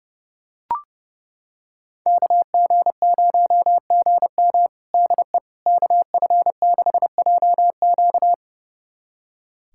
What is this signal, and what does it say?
Morse code, heard as a keyed tone, about 30 wpm, 700 Hz. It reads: KG0GM DE KF6JQ
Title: USA Callsign DE Callsign - 5 Characters - 30wpm